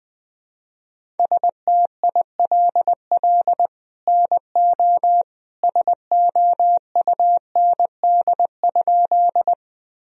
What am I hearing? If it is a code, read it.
Morse code, heard as a keyed tone, about 21 wpm, 700 Hz. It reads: STILL NO SOUND?
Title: Sentences from Top 1000 Words - Review Part 1 - 20wpm